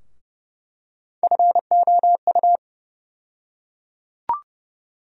Morse code, heard as a keyed tone, about 30 wpm, 700 Hz. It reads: FOU
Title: USA Call Sign Postfixes - 3 Letters 30wpm